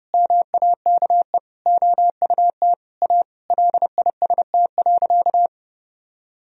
Morse code, heard as a keyed tone, about 30 wpm, 700 Hz. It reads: MAKE OUT A LIST.
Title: Sentences from Top 1000 Words - Review Part 2 - 30wpm